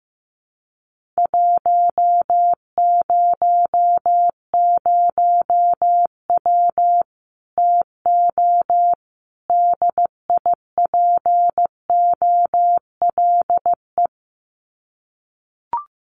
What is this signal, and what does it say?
Morse code, heard as a keyed tone, about 15 wpm, 700 Hz. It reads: 100W TO DIPOLE
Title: Common QSO Phrases 15wpm